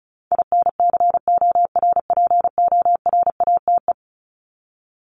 Morse code, heard as a keyed tone, about 34 wpm, 700 Hz. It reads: INCORPORATE
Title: Top 1751-2000 Words 35wpm